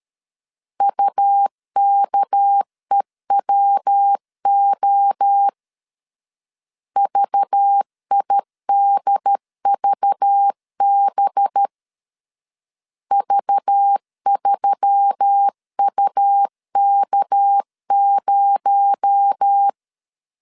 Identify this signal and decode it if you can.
Morse code, heard as a keyed tone, about 13 wpm, 800 Hz. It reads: UKEWO VIDVB V3UK0
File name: CW / DET / 1 6 1500C S B 12wpm 3431